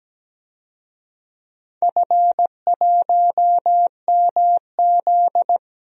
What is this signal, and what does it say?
Morse code, heard as a keyed tone, about 17 wpm, 700 Hz. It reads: F1MZ
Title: DXCC-100 Call Signs - 4 Characters - Repeated 3x - 17wpm